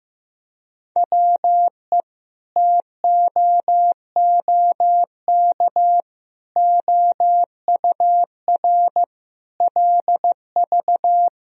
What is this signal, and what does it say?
Morse code, heard as a keyed tone, about 16 wpm, 700 Hz. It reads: WE TOOK OUR LV
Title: Sentences from Top 400 Words and Abbreviations 15wpm